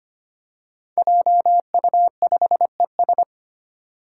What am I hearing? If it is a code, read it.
Morse code, heard as a keyed tone, about 25 wpm, 700 Hz. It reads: JU5ES
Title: Five Challenging Letter-Number Combinations 25wpm